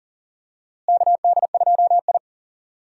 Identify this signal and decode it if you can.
Morse code, heard as a keyed tone, about 40 wpm, 700 Hz. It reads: KD2I
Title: USA Call Signs - 4 Characters - Repeated 2x - 40wpm